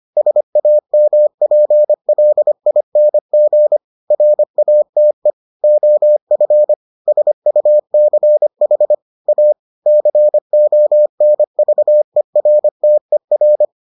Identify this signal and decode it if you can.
Morse code, heard as a keyed tone, about 25 wpm, 600 Hz. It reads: SAMPLING RATE OF SUCH A CONVERTER